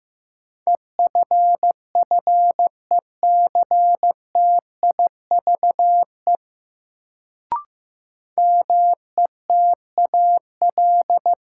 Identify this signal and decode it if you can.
Morse code, heard as a keyed tone, about 15 wpm, 700 Hz. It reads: EFFECTIVE METAL
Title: Sentences from Top 1000 Words - Review Part 6 - New Word Drill - 15wpm